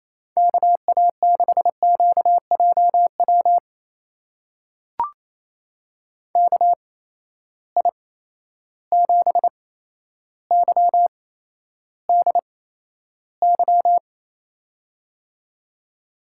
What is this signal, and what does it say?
Morse code, heard as a keyed tone, about 28 wpm, spaced as for 12 wpm, 700 Hz. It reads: KA6QJW KI7YDY
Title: USA Call Signs - 6 Characters - 8x Character Spacing - 28wpm